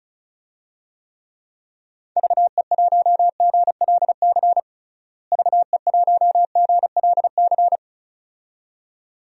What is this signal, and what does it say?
Morse code, heard as a keyed tone, about 34 wpm, 700 Hz. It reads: VE1GLC VE1GLC
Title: DXCC-100 Call Signs - 6 Characters - Repeated 2x - 35wpm